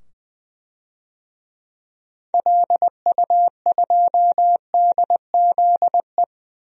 Morse code, heard as a keyed tone, about 20 wpm, 700 Hz. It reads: LU2DZE
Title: DXCC-100 Call Signs - 6 Characters - 20wpm